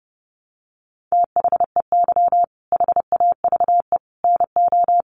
Morse code, heard as a keyed tone, about 30 wpm, 700 Hz. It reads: THEY HAVE NO
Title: Sentences from Top 1300 Words 30wpm